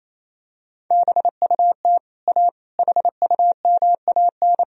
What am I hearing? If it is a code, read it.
Morse code, heard as a keyed tone, about 28 wpm, 700 Hz. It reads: BUT A HUMAN
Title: Sentences from Top 1000 Words - Review Part 3 - 28wpm